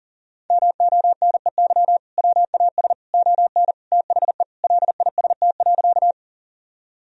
Morse code, heard as a keyed tone, about 40 wpm, 700 Hz. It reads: MONEY WAS ON THE LIST.
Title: Sentences from Top 1000 Words - Review Part 2 - 40wpm